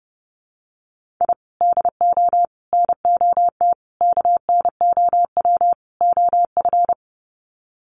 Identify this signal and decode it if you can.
Morse code, heard as a keyed tone, about 29 wpm, 700 Hz. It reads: I DO NOT KNOW OF
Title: Sets of 5 Words Encore2 30wpm